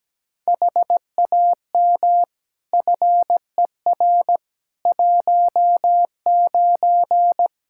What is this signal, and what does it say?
Morse code, heard as a keyed tone, about 17 wpm, 700 Hz. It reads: HAM FER 19
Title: K1USN SST QSO Practice 17wpm